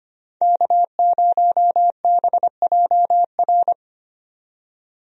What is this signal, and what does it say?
Morse code, heard as a keyed tone, about 25 wpm, 700 Hz. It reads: K0BJR
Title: USA Call Signs - 5 Characters - 25wpm